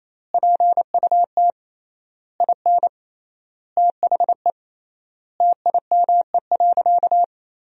Morse code, heard as a keyed tone, about 28 wpm, 700 Hz. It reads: PUT IN THE TIME.
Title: 3x Word-Spacing - Sentences from Top 200 Words - 28wpm